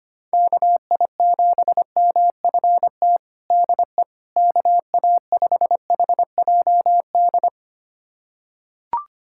Morse code, heard as a keyed tone, about 25 wpm, 700 Hz. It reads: KI7MFT DE KA5HJD
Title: USA Callsign DE Callsign - 6 Characters - 25wpm